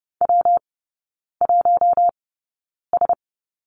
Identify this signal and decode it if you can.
Morse code, heard as a keyed tone, about 30 wpm, spaced as for 9 wpm, 700 Hz. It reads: W1S
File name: Lesson 19e - Letter w - Three Characters 30wpm